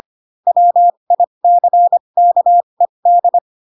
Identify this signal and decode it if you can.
Morse code, heard as a keyed tone, about 25 wpm, spaced as for 22 wpm, 700 Hz. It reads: WICKED
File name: Pinocchio / Pinocchio ch3 18 25 ik1hge